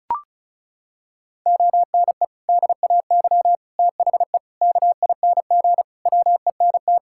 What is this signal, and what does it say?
Morse code, heard as a keyed tone, about 36 wpm, 700 Hz. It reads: ONE DAY THE KING WENT
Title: Sentences from Top 1600 Words 35wpm